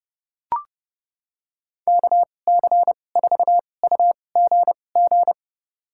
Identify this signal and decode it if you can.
Morse code, heard as a keyed tone, about 30 wpm, spaced as for 22 wpm, 700 Hz. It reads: KC4UGG
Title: USA Call Signs - 6 Characters - 2x Character Spacing - 30wpm